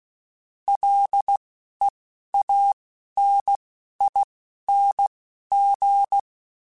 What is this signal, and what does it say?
Morse code, heard as a keyed tone, about 16 wpm, spaced as for 12 wpm, 800 Hz. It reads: LEANING
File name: Combinations / Combinations-mp3s / Combinations@10